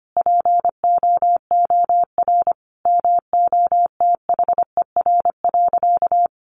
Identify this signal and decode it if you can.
Morse code, heard as a keyed tone, about 25 wpm, 700 Hz. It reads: POOR MOTHER.
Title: Sentences from Top 1000 Words 25wpm